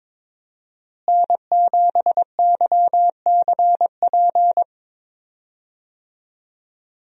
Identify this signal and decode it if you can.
Morse code, heard as a keyed tone, about 22 wpm, 700 Hz. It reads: N7YCP
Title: USA Call Signs - 5 Characters - Repeated 2x - 22wpm